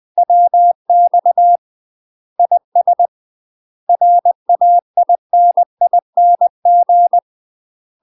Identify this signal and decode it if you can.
Morse code, heard as a keyed tone, about 20 wpm, 700 Hz. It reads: WX IS RAINING
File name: UK MASTER QSOs TRAINING / 20WPM / UK MASTER QSOs 0008